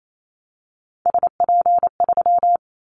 Morse code, heard as a keyed tone, about 28 wpm, 700 Hz. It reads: SP3
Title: Callsign Prefixes for Top 25 DXCC - Repeated 3x - 28wpm